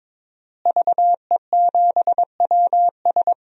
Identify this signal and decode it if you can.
Morse code, heard as a keyed tone, about 22 wpm, 700 Hz. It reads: VE7WS
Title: DXCC-100 Call Signs - 5 Characters - Repeated 2x - 22wpm